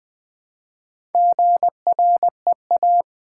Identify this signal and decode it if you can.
Morse code, heard as a keyed tone, about 20 wpm, 700 Hz. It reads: GREA
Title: Build A Word - 1 Letter At A Time - 3 to 5-Letter Words - 20wpm